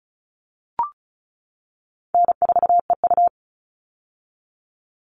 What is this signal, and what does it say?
Morse code, heard as a keyed tone, about 34 wpm, 700 Hz. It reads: N4EU
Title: All Cut Numbers - 4-Digit 35wpm